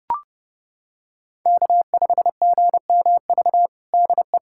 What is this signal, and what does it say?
Morse code, heard as a keyed tone, about 30 wpm, 700 Hz. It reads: K5GMV DE
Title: USA Callsign DE Callsign - 5 Characters - 30wpm